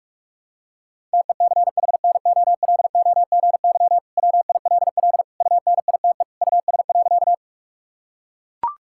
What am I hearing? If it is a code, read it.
Morse code, heard as a keyed tone, about 43 wpm, 700 Hz. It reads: TECHNOLOGY WILL UNITE US.
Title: Sentences from Top 1000 Words - Review Part 5 - 45wpm